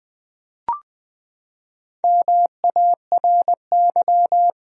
Morse code, heard as a keyed tone, about 21 wpm, 700 Hz. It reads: MARY
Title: USA Common Names - Max 5 Letters - 20wpm